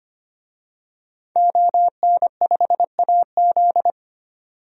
Morse code, heard as a keyed tone, about 25 wpm, 700 Hz. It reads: ON5AZ
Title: DXCC-100 Call Signs - 5 Characters - 25wpm